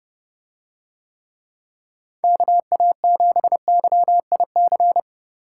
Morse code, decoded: KA7YIC

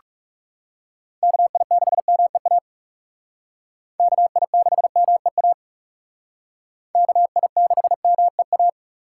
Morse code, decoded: KI6MEA KI6MEA KI6MEA

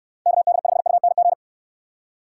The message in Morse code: KD5UNG

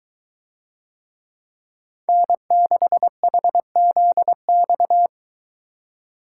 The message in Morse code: N6HZX

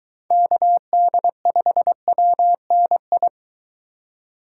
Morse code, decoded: KD5WNI

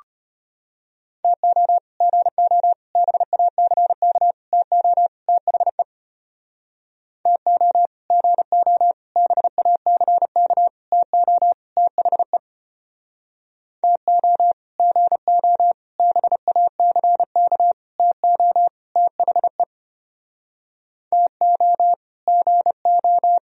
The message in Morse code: TO GO BACK TO THE TO GO BACK TO THE TO GO BACK TO THE TO GO

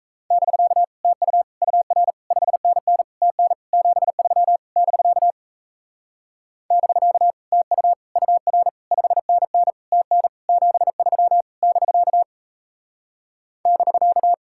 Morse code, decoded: <BK> TU UR 5NN TN 73 <BK> <BK> TU UR 5NN TN 73 <BK> <BK>

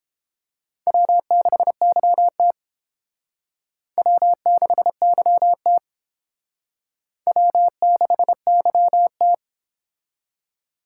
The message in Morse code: W6YT W6YT W6YT

